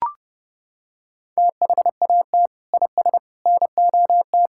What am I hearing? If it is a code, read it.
Morse code, heard as a keyed tone, about 30 wpm, 700 Hz. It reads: THAT IS NOT
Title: Sentences from Top 1000 Words - Review Part 5 - 30wpm